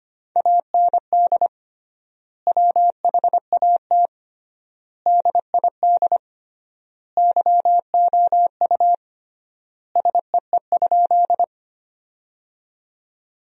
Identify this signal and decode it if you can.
Morse code, heard as a keyed tone, about 25 wpm, 700 Hz. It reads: AND WHAT DID YOU SEE?
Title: 3x Word-Spacing - Sentences from Top 100 Words - 25wpm